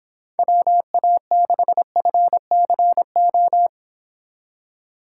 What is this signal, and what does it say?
Morse code, heard as a keyed tone, about 24 wpm, 700 Hz. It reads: WA6FCO
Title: Speed Racing - USA 2x3 Call Signs 22wpm